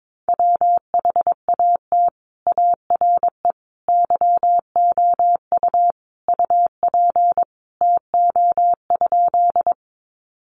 Morse code, heard as a keyed tone, about 22 wpm, 700 Hz. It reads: WHAT ARE YOU UP TO?
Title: Sentences from Top 100 Words 22wpm